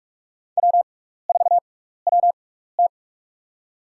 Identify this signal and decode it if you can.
Morse code, heard as a keyed tone, about 46 wpm, spaced as for 16 wpm, 700 Hz. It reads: W4WT
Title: USA Call Signs - 4 Characters - 6x Character Spacing - 45wpm